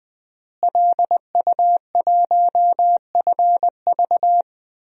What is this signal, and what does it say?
Morse code, heard as a keyed tone, about 20 wpm, 700 Hz. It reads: LU1FV
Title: DXCC-100 Call Signs - 5 Characters - Repeated 3x - 20wpm